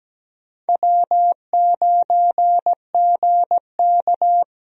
Morse code, heard as a keyed tone, about 17 wpm, 700 Hz. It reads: W9GK
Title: USA Call Signs - 4 Characters - Repeated 3x - 17wpm